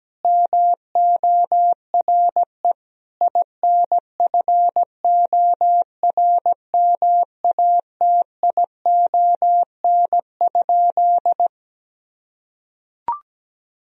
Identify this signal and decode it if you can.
Morse code, heard as a keyed tone, about 17 wpm, 700 Hz. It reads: MORE INFORMATION?
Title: Sentences from Top 1000 Words - Review Part 3 - 17wpm